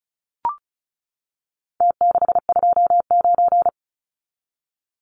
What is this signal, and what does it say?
Morse code, heard as a keyed tone, about 34 wpm, 700 Hz. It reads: T629
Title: Numbers - Contest Style - 4-Digit 35wpm